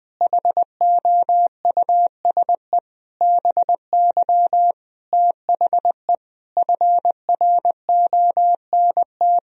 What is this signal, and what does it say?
Morse code, heard as a keyed tone, about 19 wpm, 700 Hz. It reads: HOUSE BY THE FRONT
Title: Sentences from Top 1000 Words - Review Part 2 - 20wpm